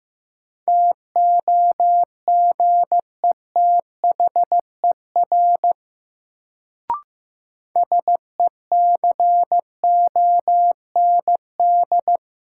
Morse code, heard as a keyed tone, about 15 wpm, 700 Hz. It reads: TOGETHER SECOND
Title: Sentences from Top 300 Words - New Word Drill - 15wpm